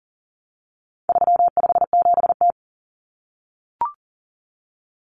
Morse code, heard as a keyed tone, about 40 wpm, 700 Hz. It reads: 357T
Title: Common Cut Numbers - 4-Digit 40wpm